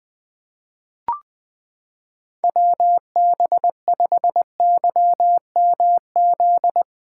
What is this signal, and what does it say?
Morse code, heard as a keyed tone, about 20 wpm, 700 Hz. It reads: WB5YMZ
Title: USA Callsign DE Callsign 20wpm